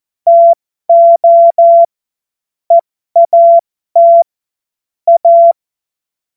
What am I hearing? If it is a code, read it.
Morse code, heard as a keyed tone, about 14 wpm, spaced as for 12 wpm, 680 Hz. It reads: TO EAT A
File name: Morsekurs / Witze